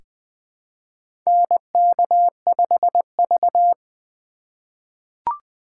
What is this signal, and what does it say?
Morse code, heard as a keyed tone, about 21 wpm, 700 Hz. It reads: NK5V
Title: USA Call Signs - 4 Characters - 2x Character Spacing - 20wpm